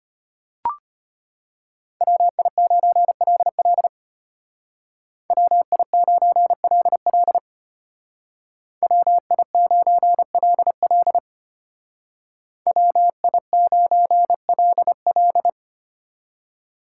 Morse code, heard as a keyed tone, about 31 wpm, 700 Hz. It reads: WI9LL WI9LL WI9LL WI9LL
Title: Speed Racing - USA Call Signs 25wpm